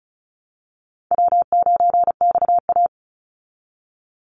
Morse code, heard as a keyed tone, about 34 wpm, 700 Hz. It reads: W9XA
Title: Easy Call Signs Repeated 3x - 35wpm